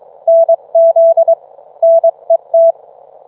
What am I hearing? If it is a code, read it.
Morse code, heard as a keyed tone, about 23 wpm, 660 Hz. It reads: NZ NET